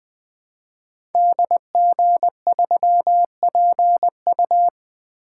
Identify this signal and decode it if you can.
Morse code, heard as a keyed tone, about 20 wpm, 700 Hz. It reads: DG3PU
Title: Code Groups - 5 Letter:Numbers - 20wpm